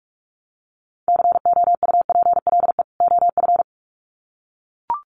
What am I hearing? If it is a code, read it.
Morse code, heard as a keyed tone, about 44 wpm, 700 Hz. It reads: COUPLE OF
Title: Sets of 2 Words 45wpm